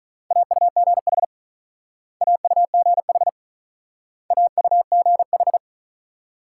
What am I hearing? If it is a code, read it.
Morse code, heard as a keyed tone, about 41 wpm, 700 Hz. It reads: AUGH AUGH AUGH
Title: Speed-Racing Multi-letter Phonemes - 35wpm